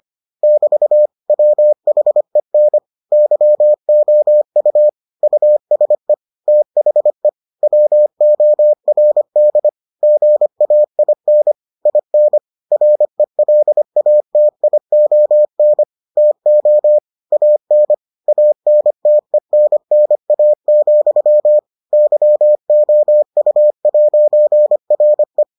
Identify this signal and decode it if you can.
Morse code, heard as a keyed tone, about 25 wpm, 600 Hz. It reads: <BT> WHEN YOU USE THE WORD GAIN IN RELATION TO AN ANTENNA, YOU'RE